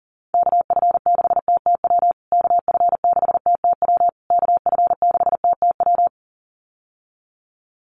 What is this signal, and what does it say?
Morse code, heard as a keyed tone, about 37 wpm, 700 Hz. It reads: KF6TTW KF6TTW KF6TTW
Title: USA Call Signs Repeated 3x 40wpm